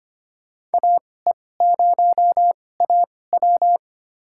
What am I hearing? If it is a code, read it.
Morse code, heard as a keyed tone, about 25 wpm, spaced as for 18 wpm, 700 Hz. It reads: AE0AW
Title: USA Call Signs - 5 Characters - 2x Character Spacing - 25wpm